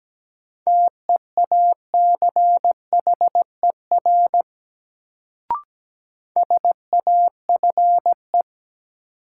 Morse code, heard as a keyed tone, about 17 wpm, 700 Hz. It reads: TEACHER SAFE
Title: Sentences from Top 900 Words - New Word Drill - 17wpm